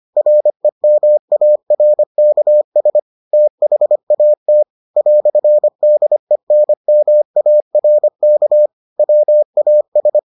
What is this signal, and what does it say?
Morse code, heard as a keyed tone, about 25 wpm, 600 Hz. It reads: REMARKS THAT "DENMARK WAS